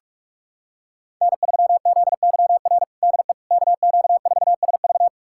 Morse code, heard as a keyed tone, about 44 wpm, 700 Hz. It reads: N3ZYR DE KQ4SV
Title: USA Callsign DE Callsign - 5 Characters - 45wpm